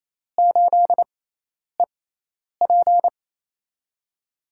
Morse code, heard as a keyed tone, about 28 wpm, spaced as for 10 wpm, 700 Hz. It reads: 8EP
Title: USA Call Signs - 6x Character Spacing - 28wpm